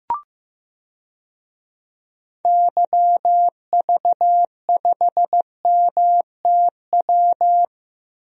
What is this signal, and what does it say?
Morse code, heard as a keyed tone, about 15 wpm, 700 Hz. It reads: YV5MTW
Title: DXCC-100 Call Signs - 6 Characters - Repeated 2x - 15wpm